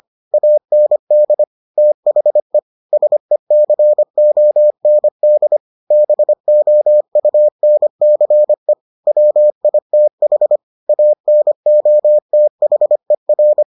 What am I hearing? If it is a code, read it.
Morse code, heard as a keyed tone, about 25 wpm, 600 Hz. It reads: AND THE SECOND BOUNCE WITH ANOTHER